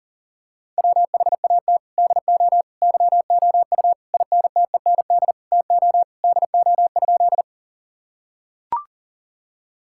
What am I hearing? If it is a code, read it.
Morse code, heard as a keyed tone, about 40 wpm, 700 Hz. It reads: WHAT DO YOU INTEND TO DO?
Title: Sentences from Top 1600 Words 40wpm